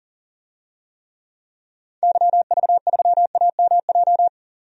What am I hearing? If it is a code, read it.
Morse code, heard as a keyed tone, about 40 wpm, 700 Hz. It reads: YV3AMJ
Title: DXCC-100 Call Signs - 6 Characters - 40wpm